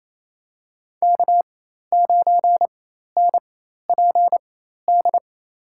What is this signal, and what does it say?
Morse code, heard as a keyed tone, about 28 wpm, spaced as for 13 wpm, 700 Hz. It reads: K9NPD
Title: USA Call Signs - 4x Character Spacing - 28wpm